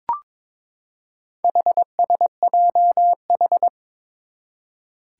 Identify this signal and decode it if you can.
Morse code, heard as a keyed tone, about 22 wpm, 700 Hz. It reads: HSJH
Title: Four Challenging Letter-Number Combinations 22wpm